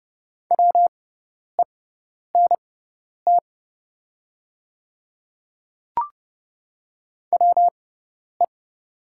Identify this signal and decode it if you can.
Morse code, heard as a keyed tone, about 30 wpm, spaced as for 10 wpm, 690 Hz. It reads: WENT WE
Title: 6x Character Spacing - Top 500 Words - 30wpm